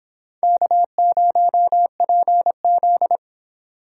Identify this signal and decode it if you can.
Morse code, heard as a keyed tone, about 30 wpm, 700 Hz. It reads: K0PZ